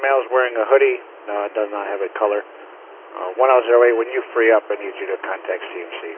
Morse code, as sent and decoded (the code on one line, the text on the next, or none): none